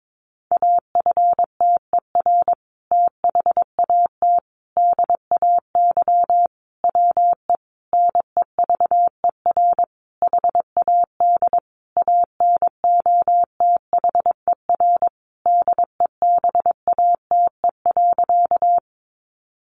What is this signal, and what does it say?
Morse code, heard as a keyed tone, about 22 wpm, 700 Hz. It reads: AFTER THAT DAY WE NEVER HAD ANOTHER DEBATE.
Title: Sentences from Top 1100 Words 22wpm